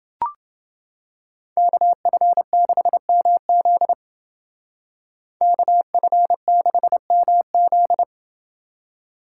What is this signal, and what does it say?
Morse code, heard as a keyed tone, about 28 wpm, 700 Hz. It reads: KF6MZ KF6MZ